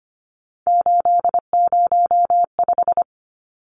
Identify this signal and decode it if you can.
Morse code, heard as a keyed tone, about 25 wpm, 700 Hz. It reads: 805
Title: Numbers - 3-Digit 25wpm